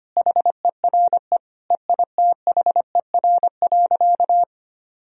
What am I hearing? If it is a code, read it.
Morse code, heard as a keyed tone, about 25 wpm, 700 Hz. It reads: HERE EITHER.
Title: Sentences from Top 1000 Words - Review Part 4 - 25wpm